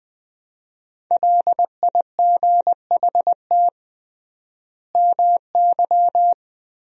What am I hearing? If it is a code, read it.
Morse code, heard as a keyed tone, about 21 wpm, 700 Hz. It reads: LIGHT MY
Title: 3x Word-Spacing - Sentences from Top 400 Words - 20wpm